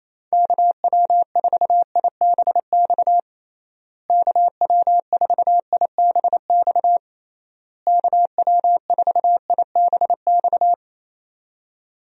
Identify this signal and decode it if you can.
Morse code, heard as a keyed tone, about 28 wpm, 700 Hz. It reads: KW4IBX KW4IBX KW4IBX
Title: USA Call Signs - 6 Characters - Repeated 3x - 28wpm